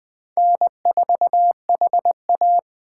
Morse code, heard as a keyed tone, about 22 wpm, 700 Hz. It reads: N4HA